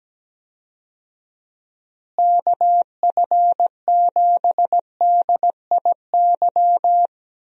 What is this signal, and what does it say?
Morse code, heard as a keyed tone, about 17 wpm, 700 Hz. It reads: KF7DIY
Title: USA Call Signs - 6 Characters - Repeated 2x - 17wpm